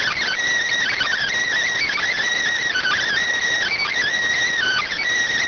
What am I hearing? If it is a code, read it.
Emergency Alert System (SAME) tones, heard as harsh buzzing bursts. It is heard from 0.9 to 4.9 s.